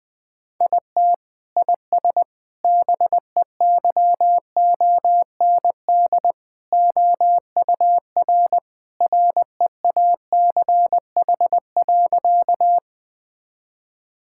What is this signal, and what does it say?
Morse code, heard as a keyed tone, about 20 wpm, 700 Hz. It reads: IT IS BEYOND OUR REACH.
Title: Sentences from Top 1000 Words - Review Part 6 - 20wpm